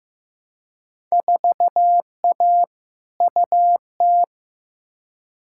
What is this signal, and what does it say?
Morse code, heard as a keyed tone, about 15 wpm, 700 Hz. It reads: 4A UT
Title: Field Day Exchanges - 15wpm